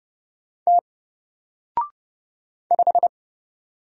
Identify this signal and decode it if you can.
Morse code, heard as a keyed tone, about 30 wpm, 700 Hz. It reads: T 5
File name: Single-Letter-Number-30wpm